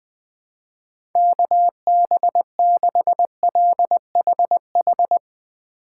Morse code, heard as a keyed tone, about 20 wpm, 700 Hz. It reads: KB6LHH
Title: USA Call Signs - 6 Characters - 2x Character Spacing - 20wpm